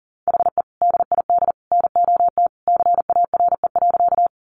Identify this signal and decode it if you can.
Morse code, heard as a keyed tone, about 40 wpm, 700 Hz. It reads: HE DID NOT CARE.
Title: Sentences from Top 600 Words 40wpm